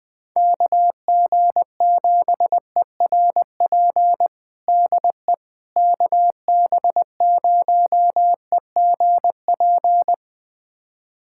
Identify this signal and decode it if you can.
Morse code, heard as a keyed tone, about 20 wpm, 700 Hz. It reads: KG7ERP DE KB0EGP
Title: USA Callsign DE Callsign 20wpm